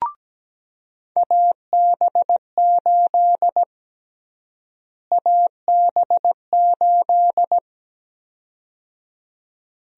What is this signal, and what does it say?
Morse code, heard as a keyed tone, about 17 wpm, 700 Hz. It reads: AB8 AB8
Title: USA Call Sign Prefixes - Repeated 2x - 17wpm